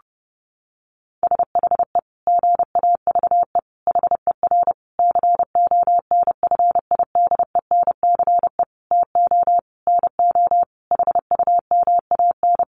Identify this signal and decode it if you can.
Morse code, heard as a keyed tone, about 30 wpm, 700 Hz. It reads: SHE GAVE HER CONFIDENCE TO NO HUMAN
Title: Sentences from Top 1500 Words 30wpm